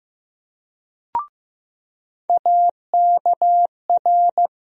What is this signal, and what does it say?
Morse code, heard as a keyed tone, about 15 wpm, 700 Hz. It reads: AKR
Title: USA Call Sign Postfixes - 3 Letters 15wpm